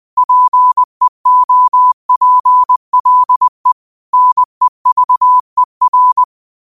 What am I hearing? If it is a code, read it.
Morse code, heard as a keyed tone, about 20 wpm, 1000 Hz. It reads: PEOPLE NEVER
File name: qotdfiles / 20wpm